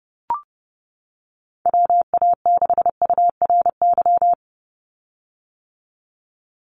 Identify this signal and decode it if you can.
Morse code, heard as a keyed tone, about 30 wpm, 700 Hz. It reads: WA6URY
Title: Call Signs - 30wpm